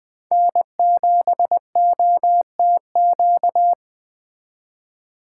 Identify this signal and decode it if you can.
Morse code, heard as a keyed tone, about 20 wpm, 700 Hz. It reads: N7OTQ
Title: Speed Racing - USA Call Signs 17wpm